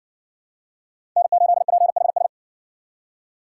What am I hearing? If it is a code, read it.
Morse code, heard as a keyed tone, about 60 wpm, 700 Hz. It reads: N8PHS